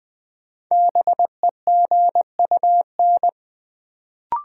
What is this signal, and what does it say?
Morse code, heard as a keyed tone, about 20 wpm, 700 Hz. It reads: BEGUN
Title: Sentences from Top 1000 Words - Review Part 6 - New Word Drill - 20wpm